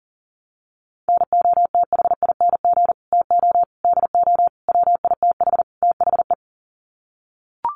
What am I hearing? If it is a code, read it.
Morse code, heard as a keyed tone, about 40 wpm, 700 Hz. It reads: NOTHING TO DO WITH THE
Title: Sets of 5 Words 40wpm